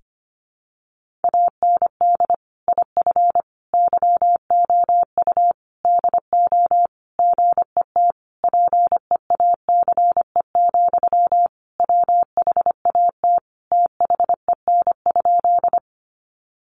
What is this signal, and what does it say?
Morse code, heard as a keyed tone, about 25 wpm, 700 Hz. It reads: AND IF YOU DO GET PEACE, WHAT THEN?
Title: Sentences from Top 1500 Words 25wpm